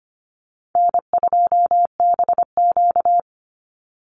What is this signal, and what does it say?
Morse code, heard as a keyed tone, about 25 wpm, 700 Hz. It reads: N2BQ
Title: USA Call Signs 25wpm